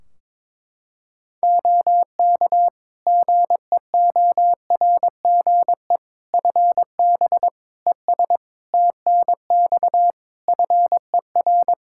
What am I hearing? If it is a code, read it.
Morse code, heard as a keyed tone, about 22 wpm, 700 Hz. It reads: OK GEORGE FB ES TNX FER
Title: K1USN SST QSO Practice 22wpm